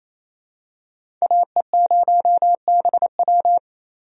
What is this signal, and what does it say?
Morse code, heard as a keyed tone, about 28 wpm, 700 Hz. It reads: AE0BW
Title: USA Call Signs - 5 Characters - 28wpm